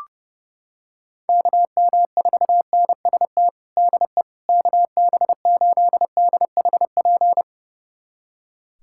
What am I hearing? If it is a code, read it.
Morse code, heard as a keyed tone, about 30 wpm, 700 Hz. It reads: KM4NST DE KB8DHP
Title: USA Callsign DE Callsign 30wpm